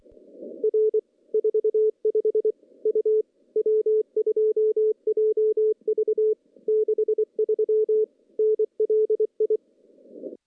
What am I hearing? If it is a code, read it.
Morse code, heard as a keyed tone, about 24 wpm, 430 Hz. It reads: R 45 U W2JV 63 NLI
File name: contests / CW-SS-k5tr- -0000